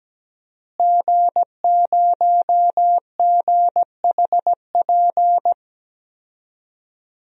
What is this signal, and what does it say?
Morse code, heard as a keyed tone, about 17 wpm, 700 Hz. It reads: G0GHP
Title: DXCC-100 Call Signs - 5 Characters - Repeated 2x - 17wpm